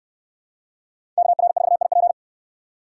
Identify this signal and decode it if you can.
Morse code, heard as a keyed tone, about 66 wpm, 700 Hz. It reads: KR4IP